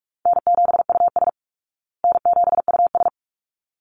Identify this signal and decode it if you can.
Morse code, heard as a keyed tone, about 44 wpm, 700 Hz. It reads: N7US N7US
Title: Call Signs Repeated 3x - 45wpm